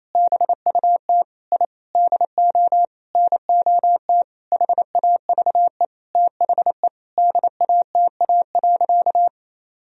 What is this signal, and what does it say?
Morse code, heard as a keyed tone, about 28 wpm, 700 Hz. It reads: BUT I DO NOT HAVE THE DATA.
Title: Sentences from Top 1000 Words - Review Part 3 - 28wpm